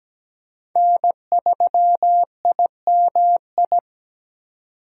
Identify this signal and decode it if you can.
Morse code, heard as a keyed tone, about 17 wpm, 700 Hz. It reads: N3IMI
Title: USA Call Signs - 5 Characters - 3x Character Spacing - 17wpm